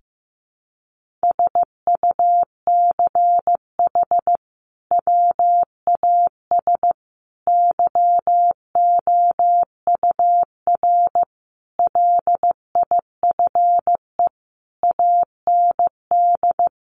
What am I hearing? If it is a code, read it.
Morse code, heard as a keyed tone, about 15 wpm, 700 Hz. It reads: SUCH WAS YOUR LIFE AND
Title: Sentences from Top 1500 Words 15wpm